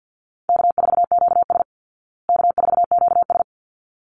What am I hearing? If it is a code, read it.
Morse code, heard as a keyed tone, about 48 wpm, 700 Hz. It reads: K4QS K4QS
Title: Call Signs Repeated 3x - 50wpm